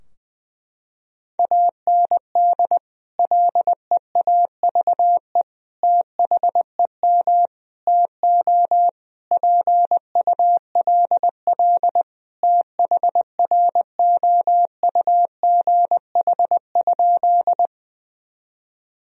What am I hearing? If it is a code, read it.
Morse code, heard as a keyed tone, about 21 wpm, 700 Hz. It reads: AND LEAVE THEM TO PULL THROUGH?
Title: Sentences from Top 1000 Words - Review Part 2 - 20wpm